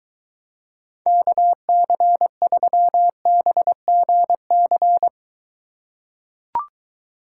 Morse code, heard as a keyed tone, about 23 wpm, 700 Hz. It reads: KC3BGC